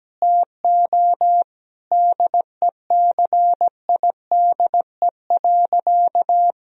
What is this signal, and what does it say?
Morse code, heard as a keyed tone, about 17 wpm, 700 Hz. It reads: TO DECIDE.
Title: Sentences from Top 1000 Words - Review Part 3 - 17wpm